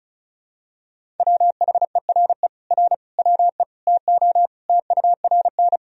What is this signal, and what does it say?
Morse code, heard as a keyed tone, about 36 wpm, 700 Hz. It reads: WHERE R WE TO TURN
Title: Sentences from Top 200 Words and Abbreviations 35wpm